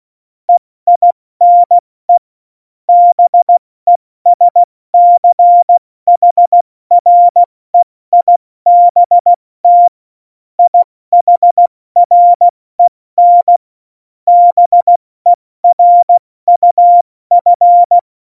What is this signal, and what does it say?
Morse code, heard as a keyed tone, about 16 wpm, 700 Hz. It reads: EINE BESCHREIBT IHREN BERUF
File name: Morsekurs / Witze / Witz 19. Januar 2021, 70 F80